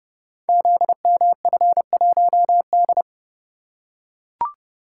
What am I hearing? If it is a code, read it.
Morse code, heard as a keyed tone, about 30 wpm, 700 Hz. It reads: ZMF1D
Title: Code Groups - 5 Letter:Numbers - 30wpm